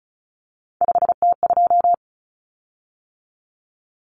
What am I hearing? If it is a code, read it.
Morse code, heard as a keyed tone, about 34 wpm, 700 Hz. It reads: 5T2
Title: Common Cut Numbers - 3-Digit 35wpm